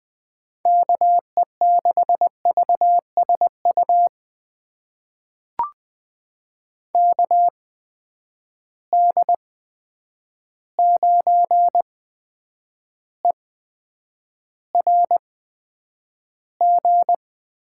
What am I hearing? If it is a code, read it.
Morse code, heard as a keyed tone, about 20 wpm, spaced as for 9 wpm, 700 Hz. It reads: KE6VSU KD9ERG
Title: USA Call Signs - 6 Characters - 8x Character Spacing - 20wpm